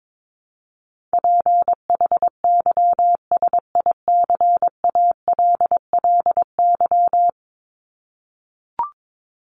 Morse code, heard as a keyed tone, about 22 wpm, 700 Hz. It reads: PHYSICALLY
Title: Top 1501-1750 Words 22wpm